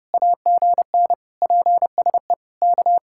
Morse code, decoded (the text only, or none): AGN PSE K